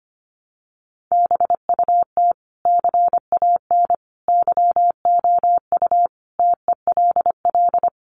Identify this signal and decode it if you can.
Morse code, heard as a keyed tone, about 25 wpm, 700 Hz. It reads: BUT CAN YOU TELL
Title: Sentences from Top 1400 Words 25wpm